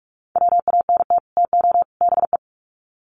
Morse code, heard as a keyed tone, about 44 wpm, 700 Hz. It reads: WANT TO BE
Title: Sets of 3 Words 45wpm